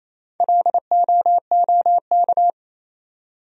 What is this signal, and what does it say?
Morse code, heard as a keyed tone, about 37 wpm, 700 Hz. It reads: LOOK